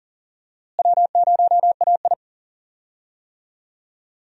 Morse code, heard as a keyed tone, about 37 wpm, 700 Hz. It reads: W0AI
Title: USA Call Signs - 4 Characters - Repeated 2x - 40wpm